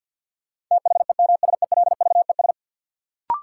Morse code, heard as a keyed tone, about 51 wpm, 700 Hz. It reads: THEMSELVES
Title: Sentences from Top 800 Words - New Word Drill - 50wpm